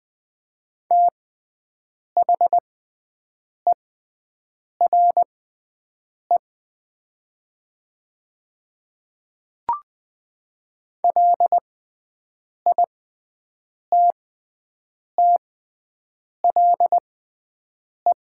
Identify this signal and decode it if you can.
Morse code, heard as a keyed tone, about 20 wpm, spaced as for 7 wpm, 700 Hz. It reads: THERE LITTLE
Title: 6x Character Spacing - Top 500 Words - 20wpm